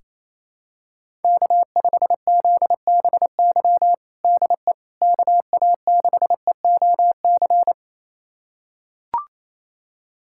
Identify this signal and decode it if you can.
Morse code, heard as a keyed tone, about 28 wpm, 700 Hz. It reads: K5ZBY DE KA6EOC
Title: USA Callsign DE Callsign 28wpm